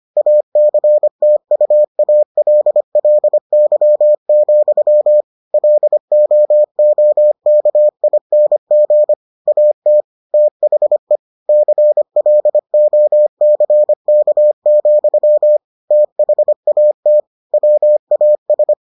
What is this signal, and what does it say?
Morse code, heard as a keyed tone, about 25 wpm, 600 Hz. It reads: ACTUALLY, LOOKING AT THE CLOCK, THAT WAS